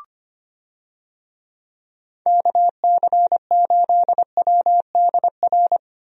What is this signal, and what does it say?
Morse code, heard as a keyed tone, about 25 wpm, 700 Hz. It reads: KC8WDR
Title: USA Call Signs - 6 Characters - Repeated 2x - 25wpm